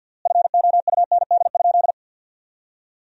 Morse code, heard as a keyed tone, about 50 wpm, 700 Hz. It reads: FOUND?
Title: 3x Word-Spacing - Sentences from Top 300 Words - 50wpm